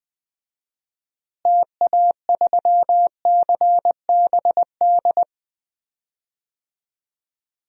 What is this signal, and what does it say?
Morse code, heard as a keyed tone, about 20 wpm, 700 Hz. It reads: TA3CBD
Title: DXCC-100 Call Signs - 6 Characters - Repeated 3x - 20wpm